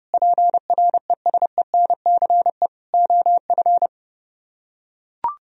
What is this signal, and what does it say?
Morse code, heard as a keyed tone, about 29 wpm, 700 Hz. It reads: PRESENCE OF